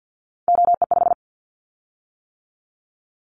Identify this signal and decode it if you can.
Morse code, heard as a keyed tone, about 50 wpm, 700 Hz. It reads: OE5
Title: Callsign Prefixes for Top 100 DXCC - 50wpm